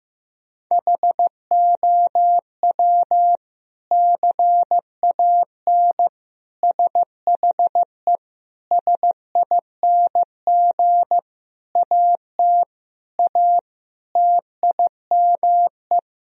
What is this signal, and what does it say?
Morse code, heard as a keyed tone, about 15 wpm, 700 Hz. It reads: HOW CAN SHE SING AT A TIME
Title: Sentences from Top 1000 Words - Review Part 2 - 15wpm